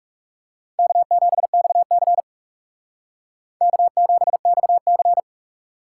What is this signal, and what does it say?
Morse code, heard as a keyed tone, about 37 wpm, 700 Hz. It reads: K7XC K7XC
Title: Speed Racing - USA 1x2 Call Signs 30wpm